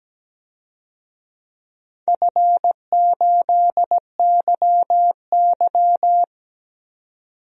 Morse code, heard as a keyed tone, about 17 wpm, 700 Hz. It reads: F8YY
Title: DXCC-100 Call Signs - 4 Characters - 17wpm